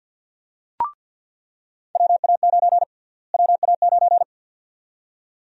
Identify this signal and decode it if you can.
Morse code, heard as a keyed tone, about 47 wpm, 700 Hz. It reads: WA9 WA9
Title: USA Call Sign Prefixes - Repeated 2x - 50wpm